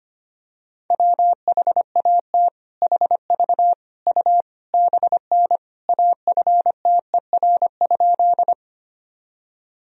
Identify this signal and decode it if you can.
Morse code, heard as a keyed tone, about 25 wpm, 700 Hz. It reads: WHAT HV U BN AFTER?
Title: Sentences from Top 200 Words and Abbreviations 25wpm